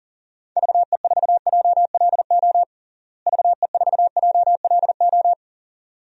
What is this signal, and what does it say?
Morse code, heard as a keyed tone, about 40 wpm, 700 Hz. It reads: VE4JLO VE4JLO
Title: DXCC-100 Call Signs - 6 Characters - Repeated 2x - 40wpm